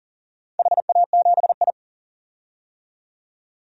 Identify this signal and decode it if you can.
Morse code, heard as a keyed tone, about 37 wpm, 700 Hz. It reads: HA7I
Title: DXCC-100 Call Signs - 4 Characters - 40wpm